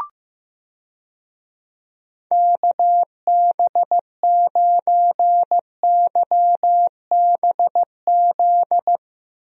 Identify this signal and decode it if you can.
Morse code, heard as a keyed tone, about 15 wpm, 700 Hz. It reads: KB9YBZ
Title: USA Call Signs - 6 Characters - Repeated 3x - 15wpm